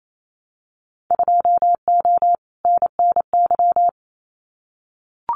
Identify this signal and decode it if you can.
Morse code, heard as a keyed tone, about 28 wpm, 700 Hz. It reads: 2O NNY
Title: Winter Field Day Exchanges 28wpm